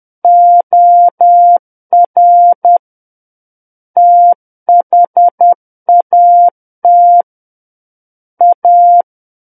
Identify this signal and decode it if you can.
Morse code, heard as a keyed tone, about 10 wpm, 700 Hz. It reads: OR THAT A
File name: Book One / 01-Chapter 1 - 10 WPM